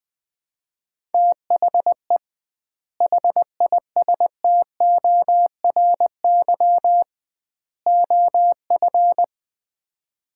Morse code, heard as a keyed tone, about 20 wpm, 700 Hz. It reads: THE HISTORY OF
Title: 2x Word-Spacing - Sets of 3 Words - 20wpm